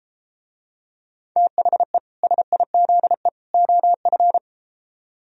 Morse code, heard as a keyed tone, about 32 wpm, 700 Hz. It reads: THE SIZE OF